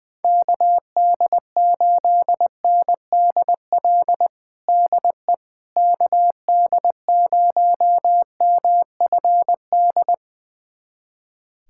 Morse code, heard as a keyed tone, about 20 wpm, 700 Hz. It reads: KD8NDL DE KD0MFD
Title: USA Callsign DE Callsign 20wpm